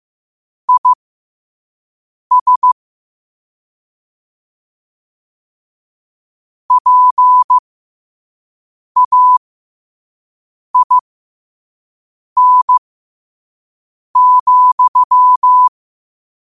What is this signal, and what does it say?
Morse code, heard as a keyed tone, about 15 wpm, spaced as for 5 wpm, 1000 Hz. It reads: IS PAIN,
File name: qotdfiles / 5wpm